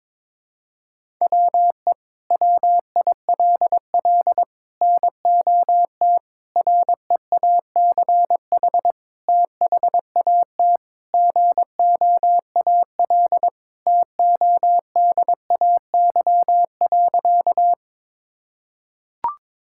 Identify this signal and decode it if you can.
Morse code, heard as a keyed tone, about 22 wpm, 700 Hz. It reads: WE WILL NOT REACH THAT GOAL TODAY.
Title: Sentences from Top 1000 Words - Review Part 5 - 22wpm